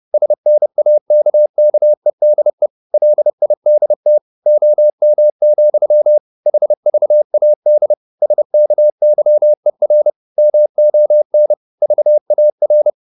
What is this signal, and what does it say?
Morse code, heard as a keyed tone, about 30 wpm, 600 Hz. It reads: SNAKKEDE LIDT OM, HVAD SKYER MON VAR
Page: Ferie 30 wpm